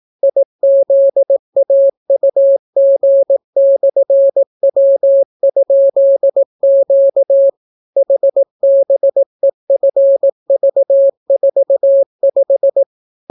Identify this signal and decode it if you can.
Morse code, heard as a keyed tone, about 18 wpm, 550 Hz. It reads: IZAUG/W?Q HBEFV45